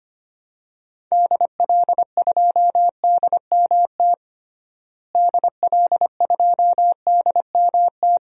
Morse code, heard as a keyed tone, about 25 wpm, 700 Hz. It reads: DL2DMT DL2DMT
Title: DXCC-100 Call Signs - 6 Characters - Repeated 2x - 25wpm